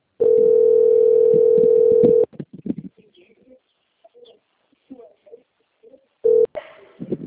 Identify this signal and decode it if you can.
Ringback tone (North America), 440 + 480 Hz.
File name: out- - - - - .219